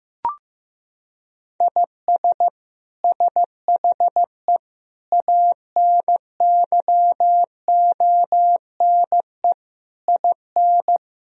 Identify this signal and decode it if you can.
Morse code, heard as a keyed tone, about 15 wpm, 700 Hz. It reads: IS SHE ANYONE IN
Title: Sentences from Top 1000 Words - Review Part 4 - 15wpm